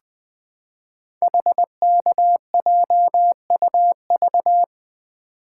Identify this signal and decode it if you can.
Morse code, heard as a keyed tone, about 21 wpm, 700 Hz. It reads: HKJUV
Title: Code Groups - 5 Letters - 20wpm